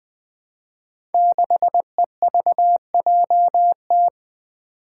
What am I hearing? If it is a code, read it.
Morse code, heard as a keyed tone, about 20 wpm, 700 Hz. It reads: 6EVJT
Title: Five Challenging Letter-Number Combinations 20wpm